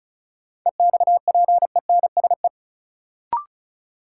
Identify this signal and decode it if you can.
Morse code, heard as a keyed tone, about 36 wpm, 700 Hz. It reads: EXPENSE
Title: Sets of 4 Words - New Word Drill - 35wpm